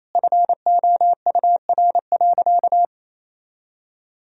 Morse code, heard as a keyed tone, about 28 wpm, 700 Hz. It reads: FOUR.
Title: Sentences from Top 1000 Words - Review Part 1 - 28wpm